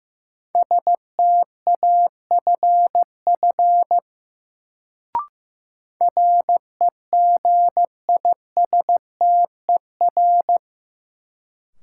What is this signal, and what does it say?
Morse code, heard as a keyed tone, about 15 wpm, 700 Hz. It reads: STAFF REGISTER
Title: Sentences from Top 1200 Words - New Word Drill - 15wpm